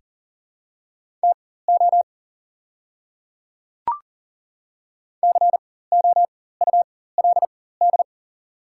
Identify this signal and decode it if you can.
Morse code, heard as a keyed tone, about 40 wpm, spaced as for 19 wpm, 700 Hz. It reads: TO COULD
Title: 4x Character Spacing - Top 100 Words - 40wpm